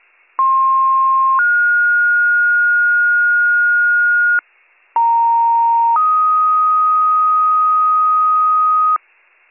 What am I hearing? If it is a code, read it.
Two-tone page, heard as a one-second tone, then a three-second tone, about 1060 then 1430 Hz, about 930 then 1250 Hz, with noise. It is heard from 0.4 to 9.0 s.